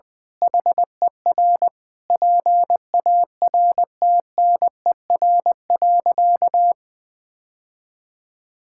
Morse code, heard as a keyed tone, about 20 wpm, 700 Hz. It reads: HER PARTNER.
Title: Sentences with 8-Word Phrases - 20wpm